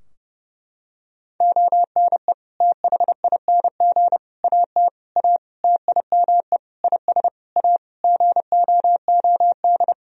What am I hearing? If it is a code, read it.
Morse code, heard as a keyed tone, about 30 wpm, 700 Hz. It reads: ONE THING AT A TIME IS A GOOD
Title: Sentences from Top 1000 Words - Review Part 2 - 30wpm